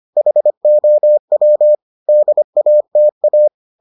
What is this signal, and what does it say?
Morse code, heard as a keyed tone, about 25 wpm, 600 Hz. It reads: HOW DATA